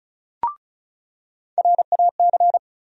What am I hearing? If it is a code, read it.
Morse code, heard as a keyed tone, about 36 wpm, 700 Hz. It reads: RAC
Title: USA Call Sign Postfixes - 3 Letters 35wpm